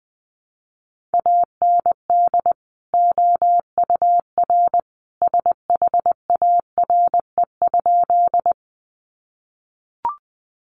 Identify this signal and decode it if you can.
Morse code, heard as a keyed tone, about 19 wpm, 700 Hz. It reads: AND OUR SHARE?
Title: Sentences from Top 900 Words 20wpm